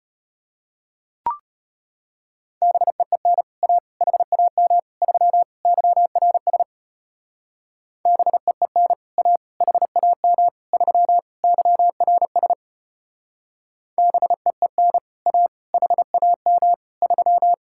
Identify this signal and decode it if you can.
Morse code, heard as a keyed tone, about 33 wpm, 700 Hz. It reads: BEEN A HAM 3 YRS BEEN A HAM 3 YRS BEEN A HAM 3